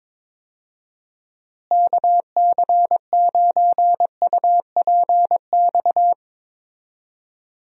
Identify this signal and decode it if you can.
Morse code, heard as a keyed tone, about 22 wpm, 700 Hz. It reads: KC9UPX
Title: USA Call Signs - 6 Characters - Repeated 2x - 22wpm